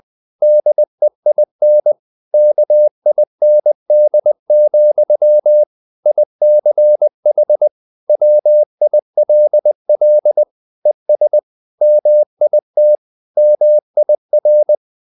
Morse code, heard as a keyed tone, about 20 wpm, 600 Hz. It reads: DEIN KIND, ICH WILL ES MIT MIR